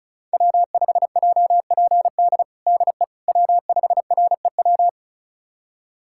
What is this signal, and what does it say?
Morse code, heard as a keyed tone, about 36 wpm, 700 Hz. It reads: W5JPD DE W5REW
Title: USA Callsign DE Callsign - 5 Characters - 35wpm